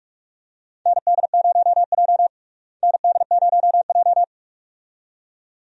Morse code, heard as a keyed tone, about 45 wpm, 700 Hz. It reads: ND0J ND0J
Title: USA Call Signs - 4 Characters - Repeated 2x - 45wpm